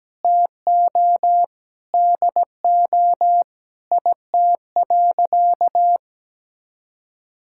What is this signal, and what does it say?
Morse code, heard as a keyed tone, about 17 wpm, 700 Hz. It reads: TO DO IT.
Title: Sentences from Top 1600 Words 17wpm